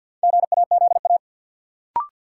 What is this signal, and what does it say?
Morse code, heard as a keyed tone, about 47 wpm, 700 Hz. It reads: GAZA
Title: Sets of 5 Words - New Word Drill - 50wpm